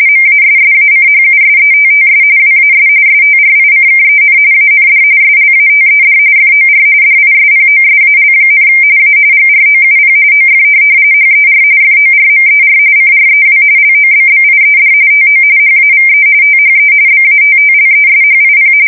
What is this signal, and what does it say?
RTTY, about 45.45 baud. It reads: MASSACHUSETTS WAS STRICKEN AND SLATED FOR DISPOSAL, A GROUP OF HER FORMER CREW LOBBIED TO HAVE THE BATTLESHIP